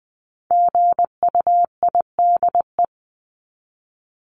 Morse code, heard as a keyed tone, about 20 wpm, 700 Hz. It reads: GUIDE
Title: Top 501-750 Words 20wpm